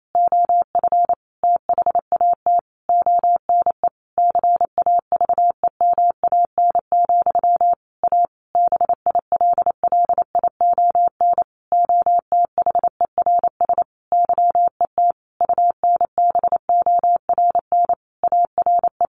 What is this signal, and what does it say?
Morse code, heard as a keyed tone, about 28 wpm, 700 Hz. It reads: OF THAT ONE CAVEMAN, A BILLION OTHERS YET UNBORN ARE
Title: A Sound Of Thunder 28wpm